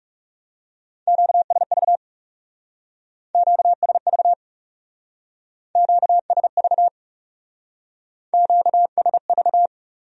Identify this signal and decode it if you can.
Morse code, heard as a keyed tone, about 36 wpm, 700 Hz. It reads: QSV QSV QSV QSV
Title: Speed-Racing Q-Codes - 30wpm